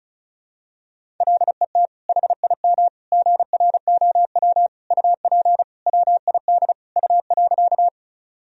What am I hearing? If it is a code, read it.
Morse code, heard as a keyed tone, about 36 wpm, 700 Hz. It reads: LET HIM GROW UP WID U.
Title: Sentences from Top 300 Words and Abbreviations 35wpm